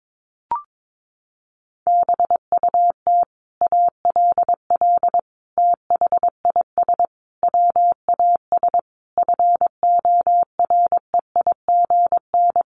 Morse code, heard as a keyed tone, about 22 wpm, 700 Hz. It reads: BUT ALL THIS WAS FOREIGN
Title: Sentences from Top 1400 Words 22wpm